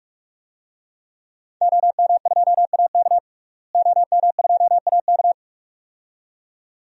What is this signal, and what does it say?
Morse code, heard as a keyed tone, about 46 wpm, 700 Hz. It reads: OM2AK OM2AK
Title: DXCC-100 Call Signs - 5 Characters - Repeated 2x - 45wpm